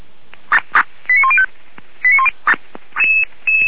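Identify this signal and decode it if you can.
Five-tone selective call (ZVEI), heard as quick runs of five tones, tones of about 70 ms. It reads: 89105
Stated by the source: ZVEI five-tone call 89138 received by SDR, recorded by feuerwehr-orlamuende (f85360) fire brigade alerting server.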